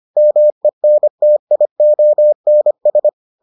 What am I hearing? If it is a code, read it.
Morse code, heard as a keyed tone, about 25 wpm, 600 Hz. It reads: MENTIONS